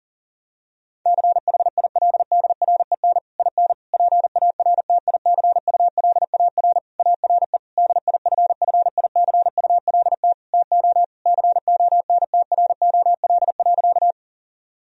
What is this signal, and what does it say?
Morse code, heard as a keyed tone, about 40 wpm, 700 Hz. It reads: CHILDREN IN PARTICULAR ARE DIFFICULT TO CONTROL.
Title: Sentences from Top 1000 Words - Review Part 4 - 40wpm